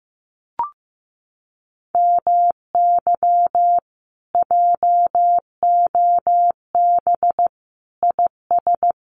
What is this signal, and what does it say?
Morse code, heard as a keyed tone, about 15 wpm, 700 Hz. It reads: MY JOB IS
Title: Common QSO Phrases Encore 15wpm